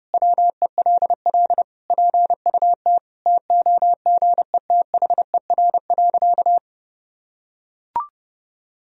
Morse code, heard as a keyed tone, about 30 wpm, 700 Hz. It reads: WELL PUT TOGETHER.
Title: Sentences from Top 1000 Words - Review Part 3 - 30wpm